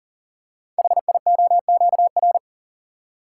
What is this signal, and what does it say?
Morse code, heard as a keyed tone, about 42 wpm, 700 Hz. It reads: HIOQR
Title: Code Groups - 5 Letter:Numbers - 40wpm